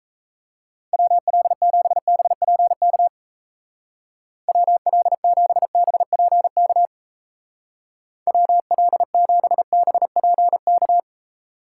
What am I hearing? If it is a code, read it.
Morse code, heard as a keyed tone, about 35 wpm, 700 Hz. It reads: WL7BPK WL7BPK WL7BPK
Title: Speed Racing - USA 2x3 Call Signs 28wpm